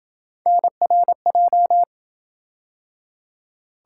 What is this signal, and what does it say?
Morse code, heard as a keyed tone, about 27 wpm, 700 Hz. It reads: NRJ